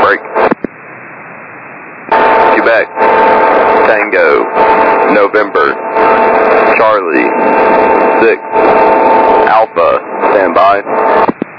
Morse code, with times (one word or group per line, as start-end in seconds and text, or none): none